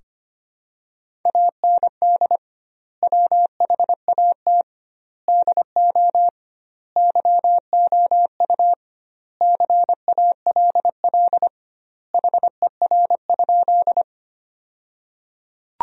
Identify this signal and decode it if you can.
Morse code, heard as a keyed tone, about 25 wpm, 700 Hz. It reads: AND WHAT DO YOU CALL HER?
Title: 2x Word-Spacing - Sentences from Top 100 Words - 25wpm